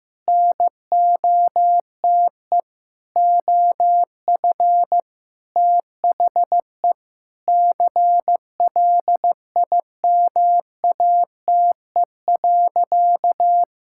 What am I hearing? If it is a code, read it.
Morse code, heard as a keyed tone, about 14 wpm, 700 Hz. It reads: NOTE OF THE CLIMATE.
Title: Sentences from Top 1600 Words 15wpm